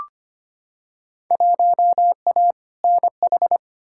Morse code, heard as a keyed tone, about 25 wpm, 700 Hz. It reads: 1A NH
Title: Field Day Exchanges Encore - 25wpm